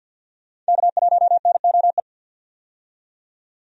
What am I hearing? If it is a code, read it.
Morse code, heard as a keyed tone, about 51 wpm, 700 Hz. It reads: K1NOE
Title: USA Call Signs - 5 Characters - 50wpm